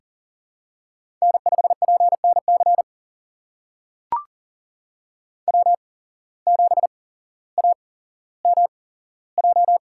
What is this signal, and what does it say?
Morse code, heard as a keyed tone, about 42 wpm, spaced as for 17 wpm, 700 Hz. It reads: N5PNC W7AMJ
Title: USA Call Signs - 5 Characters - 8x Character Spacing - 40wpm